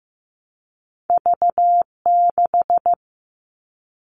Morse code, heard as a keyed tone, about 15 wpm, 700 Hz. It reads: V6
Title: All Cut Numbers - 2-Digit 15wpm